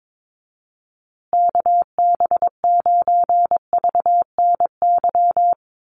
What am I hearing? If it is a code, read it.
Morse code, heard as a keyed tone, about 22 wpm, 700 Hz. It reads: KB9VNY
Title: USA Call Signs Repeated 3x 22wpm